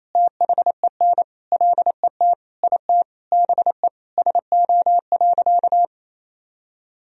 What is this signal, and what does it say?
Morse code, heard as a keyed tone, about 28 wpm, 700 Hz. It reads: THEN LET IT BE SO.
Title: Sentences from Top 1000 Words - Review Part 1 - 28wpm